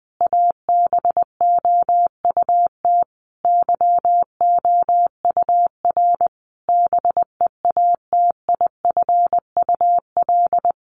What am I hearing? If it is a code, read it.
Morse code, heard as a keyed tone, about 20 wpm, 700 Hz. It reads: ABOUT YOUR BEATIFUL
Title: Sentences from Top 1300 Words 20wpm